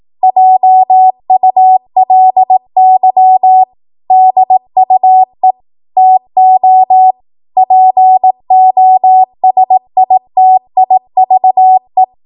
Morse code, heard as a keyed tone, about 18 wpm, 750 Hz. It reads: JULY DUE TO POSITIVE
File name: Morse / 18WPM